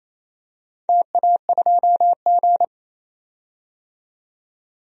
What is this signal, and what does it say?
Morse code, heard as a keyed tone, about 28 wpm, 700 Hz. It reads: TA2G
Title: DXCC-100 Call Signs - 4 Characters - 28wpm